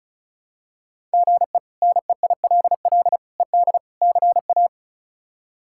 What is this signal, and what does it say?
Morse code, heard as a keyed tone, about 34 wpm, 700 Hz. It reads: GE NEILL ED CA
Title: K1USN SST QSO Practice - 35wpm